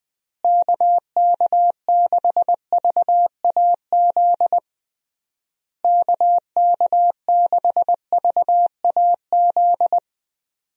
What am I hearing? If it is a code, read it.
Morse code, heard as a keyed tone, about 21 wpm, 700 Hz. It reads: KK6VAZ KK6VAZ
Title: USA Call Signs - 6 Characters - Repeated 3x - 20wpm